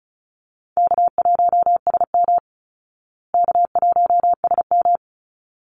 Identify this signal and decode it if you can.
Morse code, heard as a keyed tone, about 34 wpm, 700 Hz. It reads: K1SM K1SM
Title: Easy Call Signs Repeated 2x - 35wpm